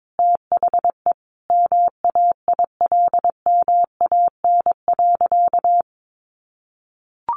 Morse code, heard as a keyed tone, about 22 wpm, 700 Hz. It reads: THE MAILMAN.
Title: Additional Easy Phrases 22wpm